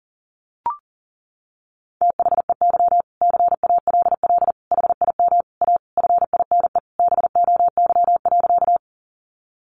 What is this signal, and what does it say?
Morse code, heard as a keyed tone, about 40 wpm, 700 Hz. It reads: THEY CALL HIM A FINE BOY.
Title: Sentences from Top 500 Words 40wpm